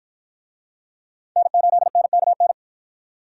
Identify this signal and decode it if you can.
Morse code, heard as a keyed tone, about 53 wpm, 700 Hz. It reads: N8NKN